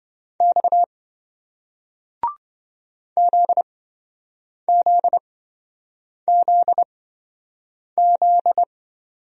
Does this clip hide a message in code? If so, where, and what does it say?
Yes: Morse code from 0.4 to 8.6 s, heard as a keyed tone, about 26 wpm, spaced as for 7 wpm, 700 Hz. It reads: X ZZZZ